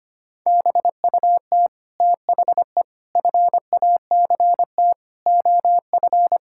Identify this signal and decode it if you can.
Morse code, heard as a keyed tone, about 25 wpm, 700 Hz. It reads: BUT THE FACT OF
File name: Sets of 9 Words - 25wpm